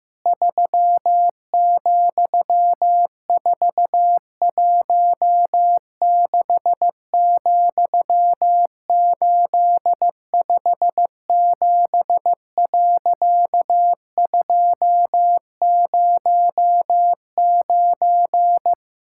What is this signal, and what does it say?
Morse code, heard as a keyed tone, about 15 wpm, 700 Hz. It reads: 3,416,857.209
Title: Pangram Challenge v2 - 15wpm